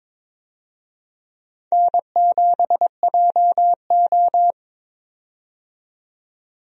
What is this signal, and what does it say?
Morse code, heard as a keyed tone, about 22 wpm, 700 Hz. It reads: N7JO
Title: USA Call Signs - 4 Characters - 22wpm